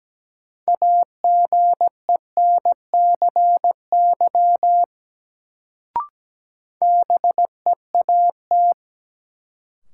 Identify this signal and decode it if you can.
Morse code, heard as a keyed tone, about 17 wpm, 700 Hz. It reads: AGENCY BEAT
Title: Sentences from Top 1300 Words - New Word Drill - 17wpm